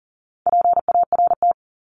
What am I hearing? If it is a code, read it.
Morse code, heard as a keyed tone, about 40 wpm, 700 Hz. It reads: PART
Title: ES:S Suffix - Top 100 Words with Base Words 40wpm